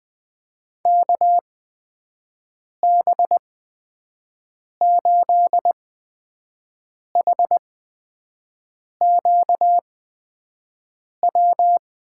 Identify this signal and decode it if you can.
Morse code, heard as a keyed tone, about 20 wpm, spaced as for 5 wpm, 700 Hz. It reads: KB8HQW
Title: USA Call Signs - 6 Characters - 8x Character Spacing - 20wpm